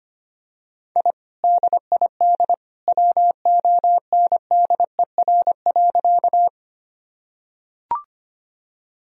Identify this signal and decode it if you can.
Morse code, heard as a keyed tone, about 25 wpm, 700 Hz. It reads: I DID WONDER.
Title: Sentences from Top 1000 Words - Review Part 3 - 25wpm